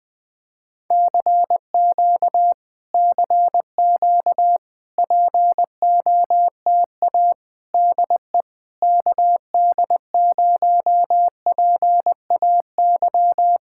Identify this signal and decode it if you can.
Morse code, heard as a keyed tone, about 20 wpm, 700 Hz. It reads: CQ CQ POTA DE KD0PAY
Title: Speed-Racing Common QSO Phrases Encore - 20wpm